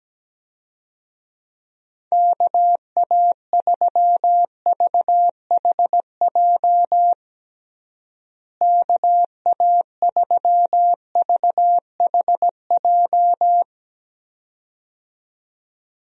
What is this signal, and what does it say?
Morse code, heard as a keyed tone, about 17 wpm, 700 Hz. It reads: KA3VHJ KA3VHJ
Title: USA Call Signs - 6 Characters - Repeated 2x - 17wpm